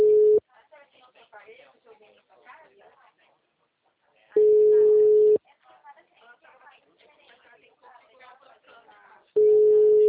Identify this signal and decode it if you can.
Ringback tone (Europe, 425 Hz), with speech.